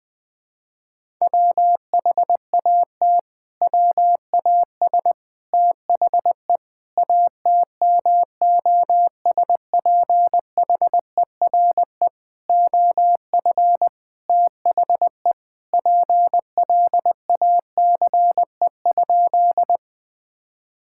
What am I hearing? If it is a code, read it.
Morse code, heard as a keyed tone, about 20 wpm, 700 Hz. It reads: WHAT WAS THE ATMOSPHERE OF THE PLACE?
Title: Sentences from Top 1600 Words 20wpm